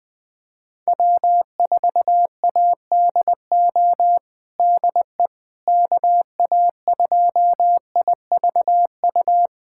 Morse code, heard as a keyed tone, about 20 wpm, 700 Hz. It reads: W4ADO DE KA2IVU
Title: USA Callsign DE Callsign 20wpm